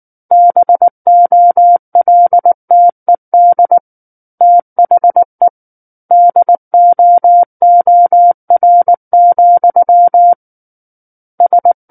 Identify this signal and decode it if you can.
Morse code, heard as a keyed tone, about 19 wpm, 700 Hz. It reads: BOLTED THE DOOR, S
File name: Book One / 10-Chapter 10 - 19 WPM